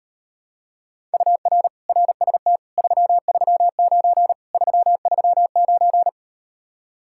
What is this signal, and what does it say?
Morse code, heard as a keyed tone, about 38 wpm, 700 Hz. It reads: UR RST 339 339